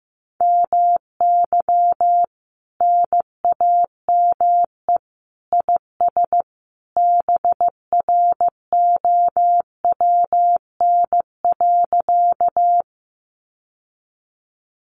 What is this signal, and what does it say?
Morse code, heard as a keyed tone, about 15 wpm, 700 Hz. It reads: MY NAME IS BROWN.
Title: Sentences from Top 1200 Words 15wpm